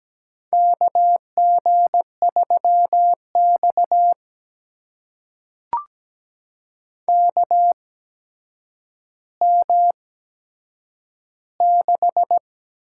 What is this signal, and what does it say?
Morse code, heard as a keyed tone, about 17 wpm, spaced as for 8 wpm, 700 Hz. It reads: KG3X KM6
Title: USA Call Signs - 8x Character Spacing - 17wpm